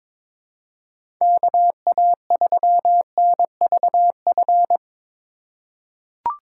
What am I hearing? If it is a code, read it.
Morse code, heard as a keyed tone, about 22 wpm, 700 Hz. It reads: KA3NVF
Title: USA Call Signs - 6 Characters - 3x Character Spacing - 22wpm